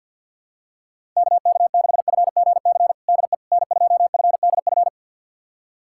Morse code, heard as a keyed tone, about 48 wpm, 700 Hz. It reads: KK6FGC DE N2FDF
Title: USA Callsign DE Callsign 50wpm